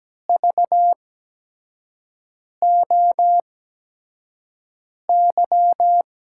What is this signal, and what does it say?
Morse code, heard as a keyed tone, about 17 wpm, spaced as for 5 wpm, 700 Hz. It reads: VOY
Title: USA Call Signs - 6 Characters - 8x Character Spacing - 17wpm